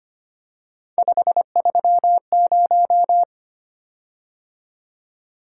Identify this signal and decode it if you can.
Morse code, heard as a keyed tone, about 25 wpm, 700 Hz. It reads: 530
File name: Three Letter-Number Combinations - 25wpm